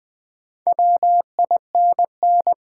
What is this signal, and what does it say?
Morse code, heard as a keyed tone, about 20 wpm, 700 Hz. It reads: WINN
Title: Build A Word - 2 Letters At A Time - 6-Letter Words - 20wpm